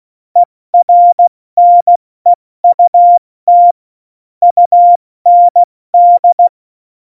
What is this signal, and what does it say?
Morse code, heard as a keyed tone, about 16 wpm, 700 Hz. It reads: ERNEUT UND
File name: Morsekurs / Witze / Witz 11. September 2018, 70 F80